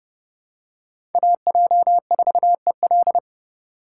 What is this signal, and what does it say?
Morse code, heard as a keyed tone, about 30 wpm, 700 Hz. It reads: AJ4EL
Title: USA Call Signs - 5 Characters - 3x Character Spacing - 30wpm